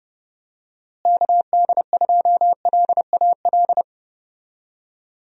Morse code, heard as a keyed tone, about 30 wpm, 700 Hz. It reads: KD2LAL
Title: USA Call Signs - 6 Characters - Repeated 2x - 30wpm